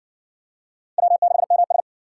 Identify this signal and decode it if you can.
Morse code, heard as a keyed tone, about 60 wpm, 700 Hz. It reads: W6MS